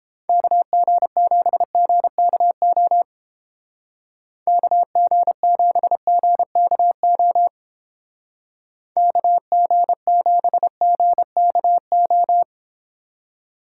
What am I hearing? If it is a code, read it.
Morse code, heard as a keyed tone, about 31 wpm, 700 Hz. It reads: KG7GKO KG7GKO KG7GKO